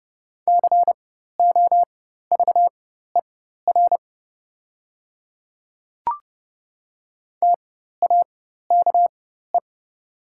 Morse code, heard as a keyed tone, about 30 wpm, spaced as for 14 wpm, 700 Hz. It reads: COVER TAKE
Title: 4x Character Spacing - Top 500 Words - 30wpm